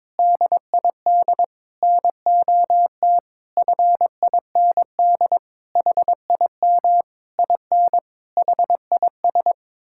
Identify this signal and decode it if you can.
Morse code, heard as a keyed tone, about 22 wpm, 700 Hz. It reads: DID NOT FIND HIM IN HIS
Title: Sentences from Top 1600 Words 22wpm